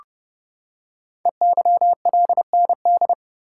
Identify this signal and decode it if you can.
Morse code, heard as a keyed tone, about 30 wpm, 690 Hz. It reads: EYLND
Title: Code Groups - 5 Letters - 30wpm